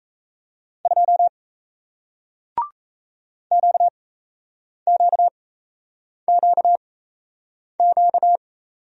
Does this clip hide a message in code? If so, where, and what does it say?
Yes: Morse code from 0.9 to 8.4 s, heard as a keyed tone, about 38 wpm, spaced as for 8 wpm, 700 Hz. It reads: 2 QQQQ